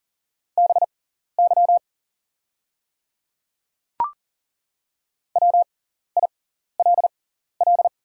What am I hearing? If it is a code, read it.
Morse code, heard as a keyed tone, about 39 wpm, spaced as for 14 wpm, 700 Hz. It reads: BY WILL
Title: 6x Character Spacing - Top 100 Words - 40wpm